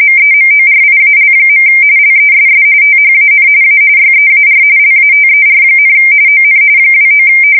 RTTY, about 45.45 baud. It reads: MEMORIAL. LESS THAN A YEAR LATER, ON 29
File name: fsk170 / USS-Missouri